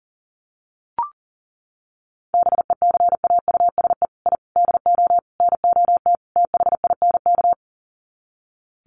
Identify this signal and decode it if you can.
Morse code, heard as a keyed tone, about 40 wpm, 700 Hz. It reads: BECAUSE I DO NOT THINK
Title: Sets of 5 Words Encore 40wpm